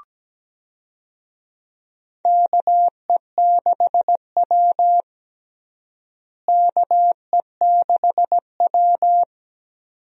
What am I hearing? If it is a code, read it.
Morse code, heard as a keyed tone, about 17 wpm, 700 Hz. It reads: KE6W KE6W
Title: USA Call Signs - 4 Characters - Repeated 2x - 17wpm